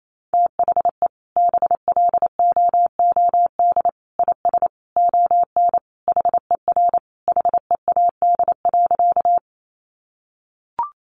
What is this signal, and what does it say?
Morse code, heard as a keyed tone, about 28 wpm, 700 Hz. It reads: THE BLOOD IS ON HER HEAD.
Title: Sentences from Top 1100 Words 28wpm